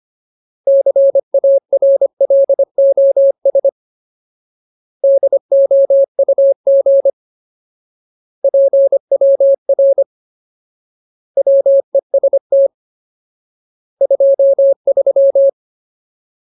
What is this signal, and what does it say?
Morse code, heard as a keyed tone, about 25 wpm, 550 Hz. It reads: CARLOS DOUG PWR WEST 23
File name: Ultimate Word List / 25WPM Ultimate